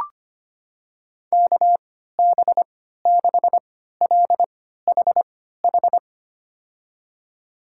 Morse code, heard as a keyed tone, about 25 wpm, spaced as for 14 wpm, 700 Hz. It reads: KB6LHH
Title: USA Call Signs - 3x Character Spacing - 25wpm